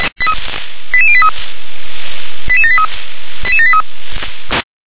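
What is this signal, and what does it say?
Five-tone selective call (ZVEI), heard as quick runs of five tones, tones of about 70 ms. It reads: 88083 88783 88783
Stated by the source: SDR-received ZVEI five-tone selective call.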